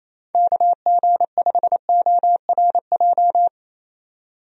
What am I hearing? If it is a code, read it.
Morse code, heard as a keyed tone, about 28 wpm, 700 Hz. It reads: KG5ORJ
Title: USA Call Signs - 6 Characters - 28wpm